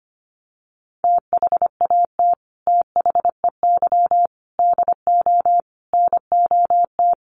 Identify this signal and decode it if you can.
Morse code, heard as a keyed tone, about 25 wpm, 700 Hz. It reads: THAT THEY DO NOT
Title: Sets of 5 Words Encore2 25wpm